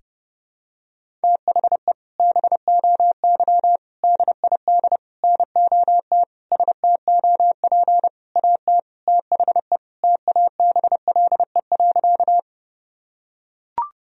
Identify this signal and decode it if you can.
Morse code, heard as a keyed tone, about 30 wpm, 700 Hz. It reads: THE BOY DID NOT STOP AT THE TABLE.
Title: Sentences from Top 1000 Words - Review Part 2 - 30wpm